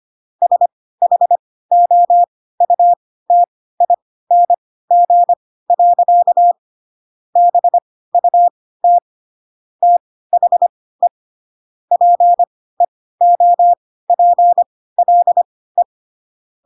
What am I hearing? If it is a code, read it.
Morse code, heard as a keyed tone, about 25 wpm, spaced as for 16 wpm, 700 Hz. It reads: SHOUTING. BUT THE PEOPLE
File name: Pinocchio / Pinocchio ch3 10 25 ik1hge